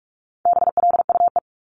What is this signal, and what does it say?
Morse code, heard as a keyed tone, about 46 wpm, 700 Hz. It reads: BLUE
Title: Multi-letter Phonemes 45wpm